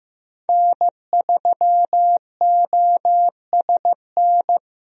Morse code, heard as a keyed tone, about 15 wpm, 700 Hz. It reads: N3OSN
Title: USA Call Signs - 5 Characters - Repeated 2x - 15wpm